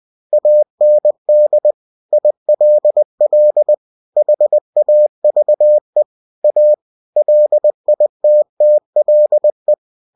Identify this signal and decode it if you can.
Morse code, heard as a keyed tone, about 20 wpm, 600 Hz. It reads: AND ILL HAVE A LITTLE
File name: cw-books / winnie-the-pooh / winnie-20wpm-page-0076